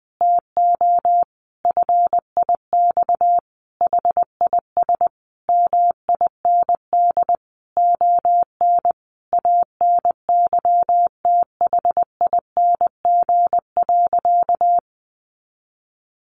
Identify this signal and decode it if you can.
Morse code, heard as a keyed tone, about 20 wpm, 700 Hz. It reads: TO FIX HIS MIND ON ANYTHING.
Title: Sentences from Top 1400 Words 20wpm